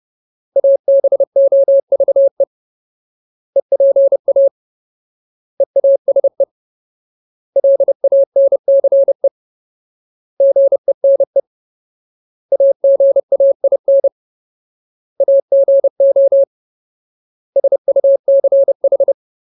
Morse code, heard as a keyed tone, about 30 wpm, 550 Hz. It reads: ABOVE EPA EASE LANCE GENE AGAIN AGO SUCH
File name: Ultimate Word List / 30WPM Ultimate